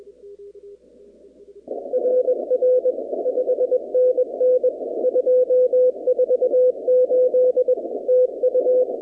Morse code, heard as a keyed tone, about 21 wpm, 500 Hz. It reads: RR 5NN 248 TU
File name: 2019-cw-wpx-kt5j-k5pi